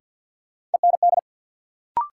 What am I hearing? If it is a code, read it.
Morse code, heard as a keyed tone, about 46 wpm, 700 Hz. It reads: END
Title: Sentences from Top 200 Words - New Word Drill - 50wpm